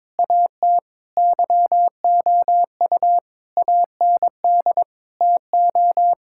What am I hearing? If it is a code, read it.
Morse code, heard as a keyed tone, about 22 wpm, 700 Hz. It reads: AT YOU AND TO
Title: Sentences from Top 1600 Words 22wpm